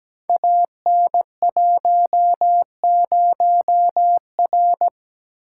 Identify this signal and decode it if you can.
Morse code, heard as a keyed tone, about 17 wpm, 700 Hz. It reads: AN10R
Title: Code Groups - 5 Letter:Numbers - 17wpm